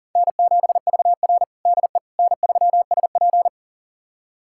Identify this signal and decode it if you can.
Morse code, heard as a keyed tone, about 37 wpm, 700 Hz. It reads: N7VR DE N3SP
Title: USA Callsign DE Callsign - 4 Characters - 40wpm